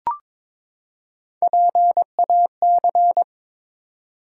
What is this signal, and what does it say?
Morse code, heard as a keyed tone, about 23 wpm, 700 Hz. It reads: PAC
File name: ARRL Sections v2 - 22wpm